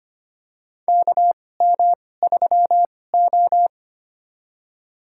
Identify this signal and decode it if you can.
Morse code, heard as a keyed tone, about 25 wpm, spaced as for 18 wpm, 700 Hz. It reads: KM3O
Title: USA Call Signs - 4 Characters - 2x Character Spacing - 25wpm